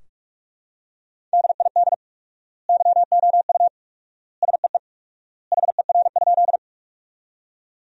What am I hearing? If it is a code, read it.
Morse code, heard as a keyed tone, about 45 wpm, 700 Hz. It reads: DID YOU SEE HER?
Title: 4x Word-Spacing - Sentences from Top 100 Words - 45wpm